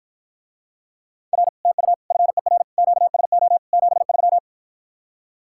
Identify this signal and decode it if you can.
Morse code, heard as a keyed tone, about 53 wpm, 700 Hz. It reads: R TU FER QSO 73